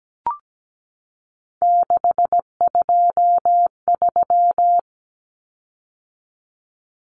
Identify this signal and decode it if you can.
Morse code, heard as a keyed tone, about 17 wpm, 700 Hz. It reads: 623
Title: Common Cut Numbers - 3-Digit 17wpm